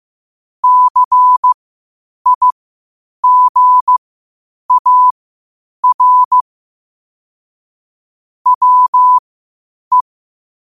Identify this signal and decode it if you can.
Morse code, heard as a keyed tone, about 15 wpm, spaced as for 8 wpm, 1000 Hz. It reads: CIGAR WE
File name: qotdfiles / 8wpm